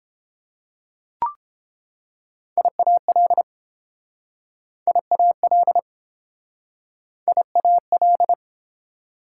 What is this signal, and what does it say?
Morse code, heard as a keyed tone, about 30 wpm, 700 Hz. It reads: IAL IAL IAL